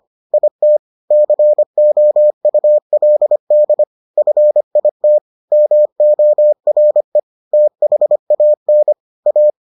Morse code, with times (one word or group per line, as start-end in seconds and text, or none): 0.3-0.8 IT
1.1-3.8 COULD
4.2-5.2 FIT
5.5-7.2 MORE
7.5-8.9 THAN
9.3-9.5 A